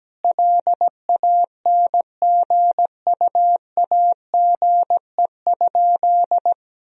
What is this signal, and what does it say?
Morse code, heard as a keyed tone, about 17 wpm, 700 Hz. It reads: LANGUAGE?
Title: Sentences from Top 1000 Words - Review Part 3 - 17wpm